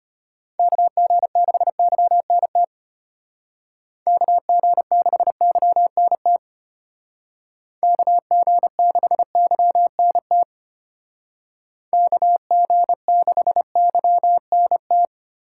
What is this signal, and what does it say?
Morse code, heard as a keyed tone, about 30 wpm, 700 Hz. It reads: KG6YNT KG6YNT KG6YNT KG6YNT
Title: Speed Racing - USA Call Signs 25wpm